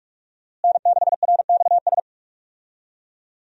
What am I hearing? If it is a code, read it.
Morse code, heard as a keyed tone, about 46 wpm, 700 Hz. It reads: N6RXS
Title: USA Call Signs - 5 Characters - Repeated 3x - 45wpm